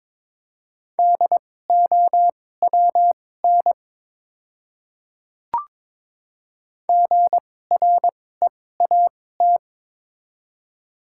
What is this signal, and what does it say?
Morse code, heard as a keyed tone, about 22 wpm, spaced as for 16 wpm, 700 Hz. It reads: DOWN GREAT
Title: 2x Character Spacing - Top 500 Words - 22wpm